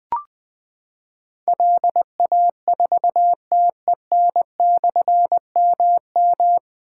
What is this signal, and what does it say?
Morse code, heard as a keyed tone, about 20 wpm, 700 Hz. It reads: LA4TEN/MM
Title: Difficult Call Signs 20wpm